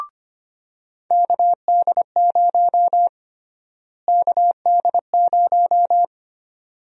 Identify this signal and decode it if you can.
Morse code, heard as a keyed tone, about 25 wpm, 700 Hz. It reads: KD0 KD0
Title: USA Call Sign Prefixes - Repeated 2x - 25wpm